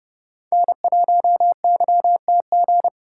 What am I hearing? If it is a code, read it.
Morse code, heard as a keyed tone, about 30 wpm, 700 Hz. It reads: N1YTG
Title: USA Call Signs - 5 Characters - 3x Character Spacing - 30wpm